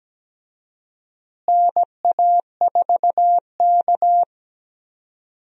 Morse code, heard as a keyed tone, about 17 wpm, 700 Hz. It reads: NA4K
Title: USA Call Signs - 4 Characters - Repeated 2x - 17wpm